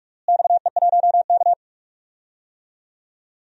Morse code, heard as a keyed tone, about 46 wpm, 700 Hz. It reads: XE1K
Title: DXCC-100 Call Signs - 4 Characters - Repeated 2x - 45wpm